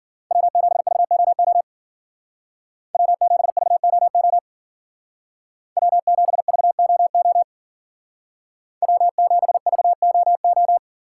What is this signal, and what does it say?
Morse code, heard as a keyed tone, about 49 wpm, 700 Hz. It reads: W7VOO W7VOO W7VOO W7VOO
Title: Speed Racing - USA 1x3 Call Signs 40wpm